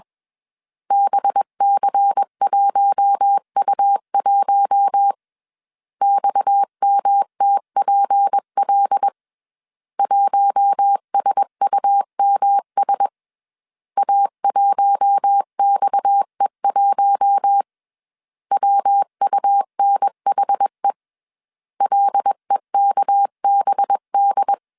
Morse code, heard as a keyed tone, about 21 wpm, 800 Hz. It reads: BC1U1 XMTPL 1SUMS A1XE1 WUNHE LEKBD
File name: CW / DET / 1 6 1500C S C 20wpm 2116